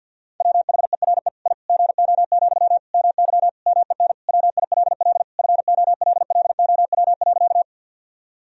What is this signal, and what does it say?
Morse code, heard as a keyed tone, about 49 wpm, 700 Hz. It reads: WHERE I GO, MY MEN WILL FOLLOW.
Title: Sentences from Top 1000 Words - Review Part 1 - 50wpm